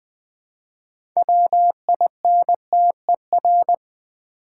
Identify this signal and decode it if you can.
Morse code, heard as a keyed tone, about 20 wpm, 700 Hz. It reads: WINTER
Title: Build A Word - 2 Letters At A Time - 6-Letter Words - 20wpm